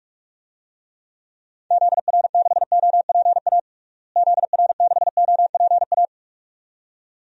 Morse code, heard as a keyed tone, about 42 wpm, 700 Hz. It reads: ZR6OPA ZR6OPA
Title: DXCC-100 Call Signs - 6 Characters - Repeated 2x - 45wpm